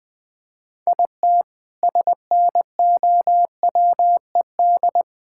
Morse code, heard as a keyed tone, about 21 wpm, 700 Hz. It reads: IT SNOWED
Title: Sentences with 6-Word Phrases - Spoken Beginning - 20wpm